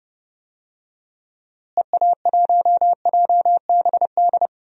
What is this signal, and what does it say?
Morse code, heard as a keyed tone, about 32 wpm, 700 Hz. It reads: EA1JBD
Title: DXCC-100 Call Signs - 6 Characters - Repeated 2x - 30wpm